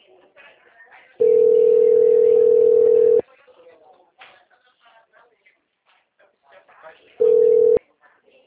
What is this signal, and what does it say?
Ringback tone (North America), 440 + 480 Hz, with speech.
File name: out- -2023- - - .44498